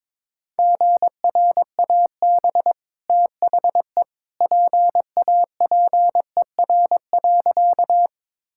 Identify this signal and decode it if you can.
Morse code, heard as a keyed tone, about 21 wpm, 690 Hz. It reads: GRAB THE PAPER.
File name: practice / 22wpm / Easy Sentences with Hints - 22wpm